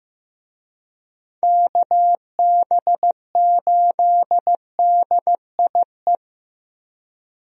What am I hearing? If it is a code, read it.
Morse code, heard as a keyed tone, about 15 wpm, 700 Hz. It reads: KB8DIE
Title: USA Call Signs - 6 Characters - 15wpm